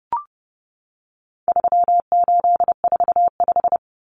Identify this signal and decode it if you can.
Morse code, heard as a keyed tone, about 30 wpm, 700 Hz. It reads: 3845
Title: Common Cut Numbers - 4-Digit 30wpm